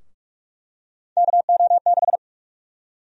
Morse code, heard as a keyed tone, about 46 wpm, 700 Hz. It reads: KO6
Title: USA Call Sign Prefixes - 45wpm